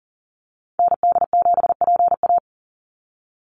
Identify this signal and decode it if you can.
Morse code, heard as a keyed tone, about 40 wpm, 700 Hz. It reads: ND7PA
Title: NA SOTA Activator Callsigns 40wpm